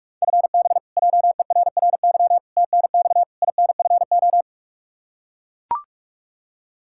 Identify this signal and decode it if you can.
Morse code, heard as a keyed tone, about 42 wpm, 700 Hz. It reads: FB JERRY TNX INFO
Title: K1USN SST QSO Practice 45wpm